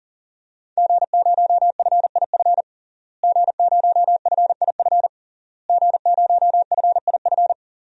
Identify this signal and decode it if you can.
Morse code, heard as a keyed tone, about 40 wpm, 700 Hz. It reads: G0FIF G0FIF G0FIF
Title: DXCC-100 Call Signs - 5 Characters - Repeated 3x - 40wpm